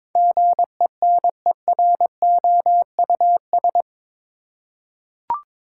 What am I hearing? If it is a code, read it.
Morse code, heard as a keyed tone, about 22 wpm, 700 Hz. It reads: GENEROUS
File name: practice / 22wpm / Eight Letter Words - 22wpm